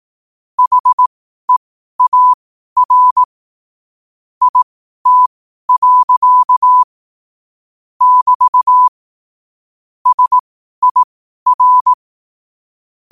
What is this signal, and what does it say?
Morse code, heard as a keyed tone, about 18 wpm, spaced as for 12 wpm, 1000 Hz. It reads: HEAR IT. <BT> SIR